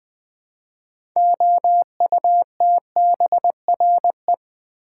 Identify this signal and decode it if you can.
Morse code, heard as a keyed tone, about 20 wpm, 700 Hz. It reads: OUTBRE
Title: Build A Word - 2 Letters At A Time - 8-Letter Words - 20wpm